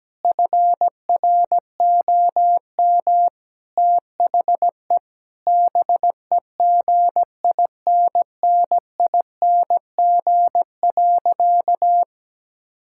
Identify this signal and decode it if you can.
Morse code, heard as a keyed tone, about 17 wpm, 700 Hz. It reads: FROM THE BEGINNING.
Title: Sentences from Top 1000 Words - Review Part 5 - 17wpm